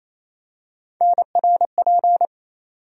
Morse code, heard as a keyed tone, about 28 wpm, 700 Hz. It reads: NRP
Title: USA Call Sign Postfixes - 3 Letters 28wpm